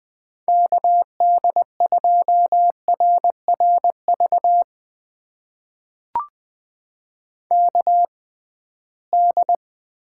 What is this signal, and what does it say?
Morse code, heard as a keyed tone, about 20 wpm, spaced as for 15 wpm, 700 Hz. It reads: KD2RRV KD
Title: USA Call Signs - 6 Characters - 6x Character Spacing - 20wpm